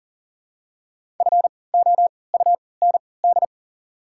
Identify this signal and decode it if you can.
Morse code, heard as a keyed tone, about 39 wpm, spaced as for 22 wpm, 700 Hz. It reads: FOUND
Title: 3x Character Spacing - Top 300 Words - 40wpm